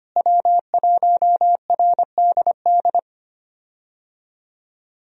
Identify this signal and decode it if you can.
Morse code, heard as a keyed tone, about 25 wpm, 700 Hz. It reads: W1RDD
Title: USA Call Signs - 5 Characters - Repeated 3x - 25wpm